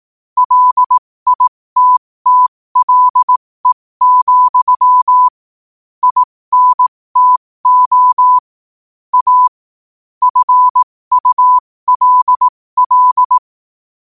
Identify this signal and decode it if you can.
Morse code, heard as a keyed tone, about 18 wpm, spaced as for 15 wpm, 1000 Hz. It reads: LITTLE, INTO A FULL
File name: qotdfiles / 15wpm